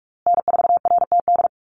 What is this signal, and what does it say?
Morse code, heard as a keyed tone, about 44 wpm, 690 Hz. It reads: N4RTD
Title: USA Call Signs Repeated 3x 45wpm